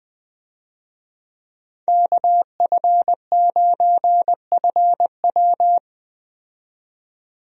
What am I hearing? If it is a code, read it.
Morse code, heard as a keyed tone, about 20 wpm, 700 Hz. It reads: KF9FW
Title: USA Call Signs - 5 Characters - 20wpm